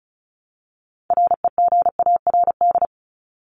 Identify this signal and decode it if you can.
Morse code, heard as a keyed tone, about 34 wpm, 700 Hz. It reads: REGARD
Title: ED Suffix - Top 100 Words with Base Words 35wpm